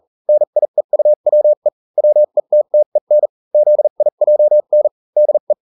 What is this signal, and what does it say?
Morse code, heard as a keyed tone, about 40 wpm, spaced as for 35 wpm, 600 Hz. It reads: NIEUWE WETTEN ZIJN DE